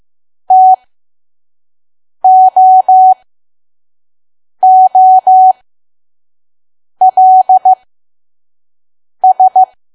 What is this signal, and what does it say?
Morse code, heard as a keyed tone, about 15 wpm, spaced as for 5 wpm, 750 Hz. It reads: TOOLS
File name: Archive / 5 WPM files / PM